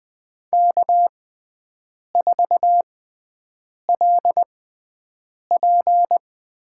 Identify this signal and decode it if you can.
Morse code, heard as a keyed tone, about 20 wpm, spaced as for 7 wpm, 700 Hz. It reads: K4LP
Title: USA Call Signs - 5 Characters - 6x Character Spacing - 20wpm